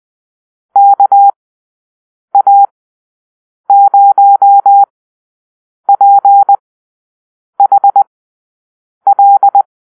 Morse code, heard as a keyed tone, about 20 wpm, spaced as for 7 wpm, 800 Hz. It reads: KA0PHL
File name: arrl / Call-Sign Drill-8